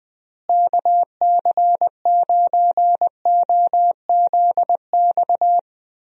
Morse code, heard as a keyed tone, about 20 wpm, 700 Hz. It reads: KC9OZX
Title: USA Call Signs - 6 Characters - Repeated 2x - 20wpm